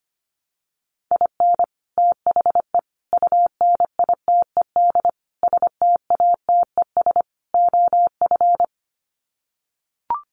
Morse code, heard as a keyed tone, about 25 wpm, 700 Hz. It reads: IN THE UNITED STATES OF
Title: Sets of 5 Words Encore2 25wpm